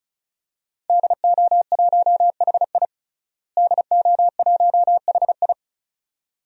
Morse code, heard as a keyed tone, about 34 wpm, 700 Hz. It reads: DO1HI DO1HI
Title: DXCC-100 Call Signs - 5 Characters - Repeated 2x - 35wpm